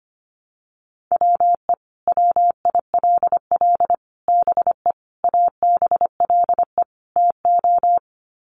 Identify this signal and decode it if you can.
Morse code, heard as a keyed tone, about 25 wpm, 700 Hz. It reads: WE WILL BE ABLE TO
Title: Sets of 5 Words Encore 25wpm